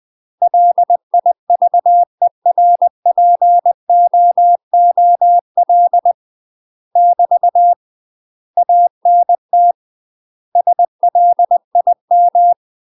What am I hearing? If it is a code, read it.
Morse code, heard as a keyed tone, about 20 wpm, 700 Hz. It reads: LIVERPOOL <BT> ANT SLIM
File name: UK MASTER QSOs TRAINING/20WPM